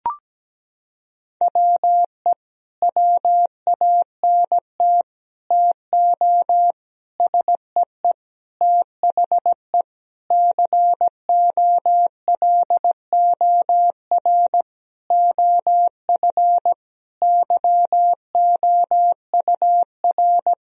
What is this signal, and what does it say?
Morse code, heard as a keyed tone, about 17 wpm, 700 Hz. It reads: WE WANT TO SEE THE COLOR OF YOUR
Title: Sentences from Top 1000 Words - Review Part 2 - 17wpm